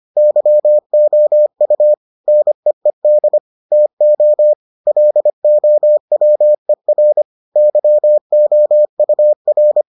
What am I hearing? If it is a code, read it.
Morse code, heard as a keyed tone, about 25 wpm, 600 Hz. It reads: YOU NEED TO LOWER YOUR